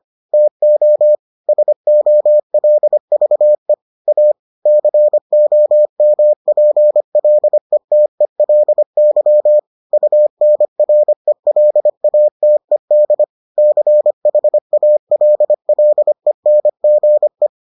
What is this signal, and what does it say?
Morse code, heard as a keyed tone, about 25 wpm, 600 Hz. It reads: TO SOLVE A COMPLETELY UNRELATED CHALLENGE